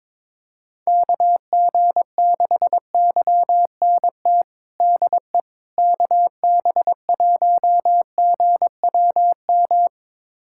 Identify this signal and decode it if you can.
Morse code, heard as a keyed tone, about 22 wpm, 700 Hz. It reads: KG6YNT DE KB1GWM
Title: USA Callsign DE Callsign - 6 Characters - 22wpm